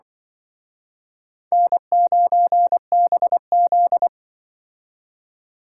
Morse code, heard as a keyed tone, about 24 wpm, 700 Hz. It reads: N9BZ
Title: Speed Racing - USA 1x2 Call Signs 20wpm